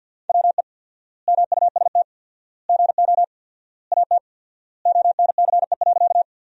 Morse code, heard as a keyed tone, about 50 wpm, 700 Hz. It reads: WE MUST GO AT ONCE.
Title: 4x Word-Spacing - Sentences from Top 300 Words - 50wpm